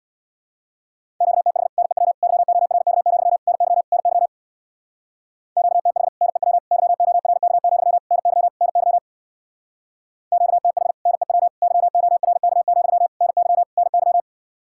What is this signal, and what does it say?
Morse code, heard as a keyed tone, about 62 wpm, 700 Hz. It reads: QTH NEW YORK, NY NY QTH NEW YORK, NY NY QTH NEW YORK, NY NY
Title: Speed-Racing Common QSO Phrases - 50wpm